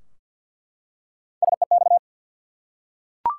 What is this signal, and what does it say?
Morse code, heard as a keyed tone, about 49 wpm, 700 Hz. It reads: SEX
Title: Three Letter Words - 50wpm